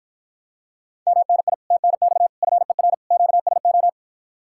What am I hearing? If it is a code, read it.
Morse code, heard as a keyed tone, about 52 wpm, 700 Hz. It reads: MNI TNX FER QSO